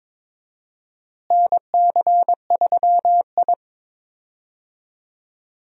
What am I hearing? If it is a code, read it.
Morse code, heard as a keyed tone, about 22 wpm, 700 Hz. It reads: NC3I
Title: USA Call Signs - 4 Characters - Repeated 2x - 22wpm